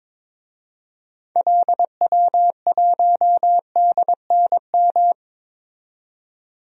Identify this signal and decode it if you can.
Morse code, heard as a keyed tone, about 22 wpm, 700 Hz. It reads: LW1DNM
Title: DXCC-100 Call Signs - 6 Characters - 22wpm